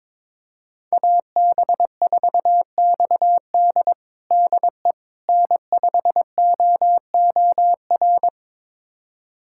Morse code, heard as a keyed tone, about 22 wpm, 700 Hz. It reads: AB4XD DE N5OOR
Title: USA Callsign DE Callsign - 5 Characters - 22wpm